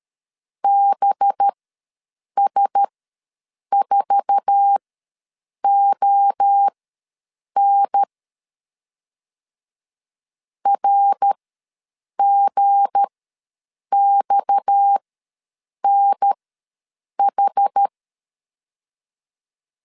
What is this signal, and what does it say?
Morse code, heard as a keyed tone, about 13 wpm, spaced as for 7 wpm, 800 Hz. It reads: BS4ON RGXNH
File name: LEZIONI / 8 YJ45 / 8 YJ4 ws 12wpm 3249